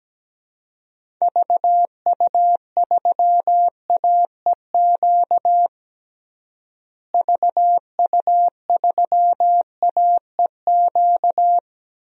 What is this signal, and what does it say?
Morse code, heard as a keyed tone, about 17 wpm, 700 Hz. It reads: VU3AEQ VU3AEQ
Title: DXCC-100 Call Signs - 6 Characters - Repeated 3x - 17wpm